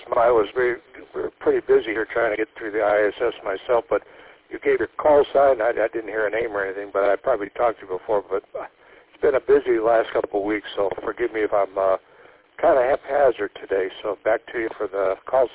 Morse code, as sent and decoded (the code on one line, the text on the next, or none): none